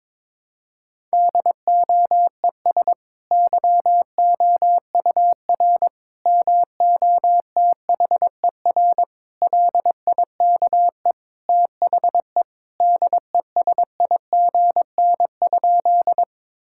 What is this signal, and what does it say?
Morse code, heard as a keyed tone, about 22 wpm, 700 Hz. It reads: DOES YOUR MOTHER LIKE THE DESIGN?
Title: Sentences from Top 1000 Words - Review Part 5 - 22wpm